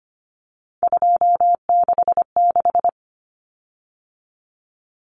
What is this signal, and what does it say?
Morse code, heard as a keyed tone, about 25 wpm, 700 Hz. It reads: 266
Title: Common Cut Numbers - 3-Digit 25wpm